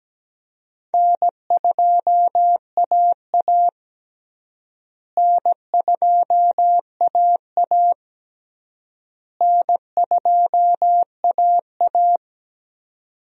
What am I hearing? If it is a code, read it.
Morse code, heard as a keyed tone, about 17 wpm, 700 Hz. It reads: N2AA N2AA N2AA
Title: USA Call Signs - 4 Characters - Repeated 3x - 17wpm